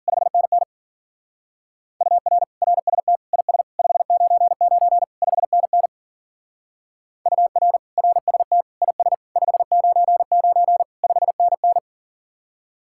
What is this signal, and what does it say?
Morse code, heard as a keyed tone, about 46 wpm, 700 Hz. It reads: 5NN UR RST IS 599 5NN UR RST IS 599 5NN
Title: Speed-Racing Common QSO Phrases - 40wpm